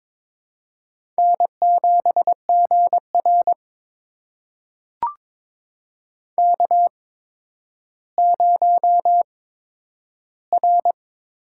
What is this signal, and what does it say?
Morse code, heard as a keyed tone, about 22 wpm, spaced as for 11 wpm, 700 Hz. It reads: N7GR K0R
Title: USA Call Signs - 4 Characters - 8x Character Spacing - 22wpm